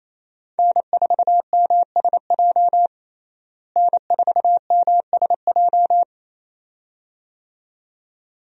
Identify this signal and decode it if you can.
Morse code, heard as a keyed tone, about 28 wpm, 700 Hz. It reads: N4MSJ N4MSJ
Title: USA Call Signs - 5 Characters - Repeated 2x - 28wpm